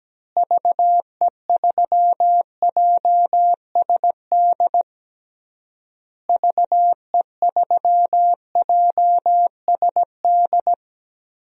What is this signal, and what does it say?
Morse code, heard as a keyed tone, about 17 wpm, 700 Hz. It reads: VE3JSD VE3JSD
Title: DXCC-100 Call Signs - 6 Characters - Repeated 3x - 17wpm